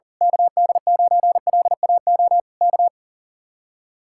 Morse code, heard as a keyed tone, about 37 wpm, 700 Hz. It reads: KD9LAO K
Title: Speed-Racing Common QSO Phrases Encore - 40wpm